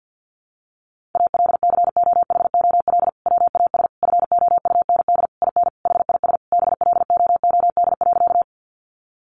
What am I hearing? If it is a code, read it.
Morse code, heard as a keyed tone, about 50 wpm, 700 Hz. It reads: ALCOHOL WAS FOUND IN HIS BLOOD.
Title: Sentences from Top 1400 Words 50wpm